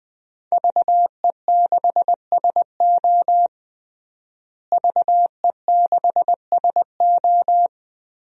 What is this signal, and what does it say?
Morse code, heard as a keyed tone, about 21 wpm, 700 Hz. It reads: VE6SO VE6SO
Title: DXCC-100 Call Signs - 5 Characters - Repeated 3x - 20wpm